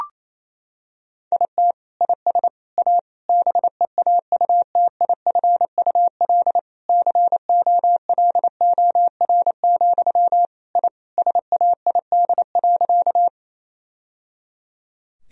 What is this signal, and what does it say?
Morse code, heard as a keyed tone, about 28 wpm, 700 Hz. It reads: IT IS A BEAUTIFUL COLOR, I SAID.
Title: Sentences from Top 1000 Words - Review Part 2 - 28wpm